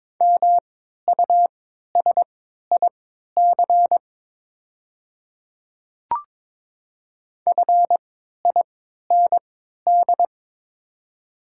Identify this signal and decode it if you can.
Morse code, heard as a keyed tone, about 23 wpm, spaced as for 13 wpm, 700 Hz. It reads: MUSIC FIND
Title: 3x Character Spacing - Top 300 Words - 22wpm